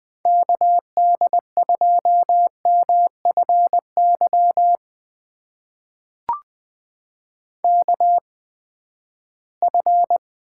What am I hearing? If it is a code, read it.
Morse code, heard as a keyed tone, about 21 wpm, spaced as for 14 wpm, 700 Hz. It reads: KD2MFY KF
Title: USA Call Signs - 6 Characters - 8x Character Spacing - 20wpm